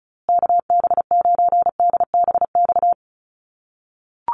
Morse code, heard as a keyed tone, about 34 wpm, 700 Hz. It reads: KB9DBX
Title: USA Call Signs Repeated 3x 35wpm